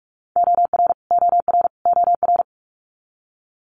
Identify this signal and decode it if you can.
Morse code, heard as a keyed tone, about 46 wpm, 700 Hz. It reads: OR OR OR
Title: Multi-letter Phonemes Repeated 3x 45wpm